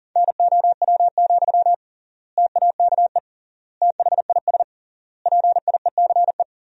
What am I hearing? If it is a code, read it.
Morse code, heard as a keyed tone, about 40 wpm, 700 Hz. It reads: NOW, TAKE THIS PIECE
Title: 3x Word-Spacing - Sentences from Top 400 Words - 40wpm